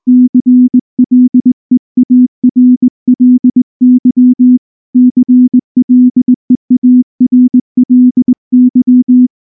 Morse code, heard as a keyed tone, about 19 wpm, 250 Hz. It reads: CLEARLY CLEARLY
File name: morse / morse-training-20WPM-1